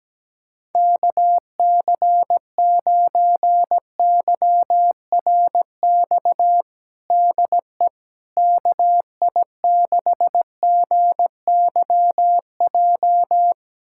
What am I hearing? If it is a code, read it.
Morse code, heard as a keyed tone, about 17 wpm, 700 Hz. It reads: KC9YRX DE KI6GYJ
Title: USA Callsign DE Callsign 17wpm